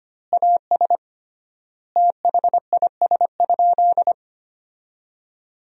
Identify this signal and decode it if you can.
Morse code, heard as a keyed tone, about 25 wpm, 700 Hz. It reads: AS THIS?
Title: 3x Word-Spacing - Sentences from Top 400 Words - 25wpm